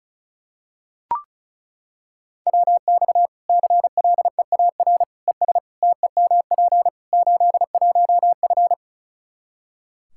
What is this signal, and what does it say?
Morse code, heard as a keyed tone, about 36 wpm, 700 Hz. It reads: WX CLEAR ES TEMP 81F
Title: Practice QSO - Exchange 2 - Info - 35wpm